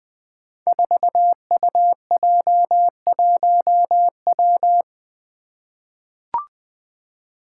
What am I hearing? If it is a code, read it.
Morse code, heard as a keyed tone, about 20 wpm, 700 Hz. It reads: 4UJ1W
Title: Five Challenging Letter-Number Combinations 20wpm